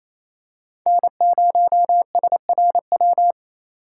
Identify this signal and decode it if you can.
Morse code, heard as a keyed tone, about 27 wpm, 700 Hz. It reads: N0SRW
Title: USA Call Signs - 3x Character Spacing - 28wpm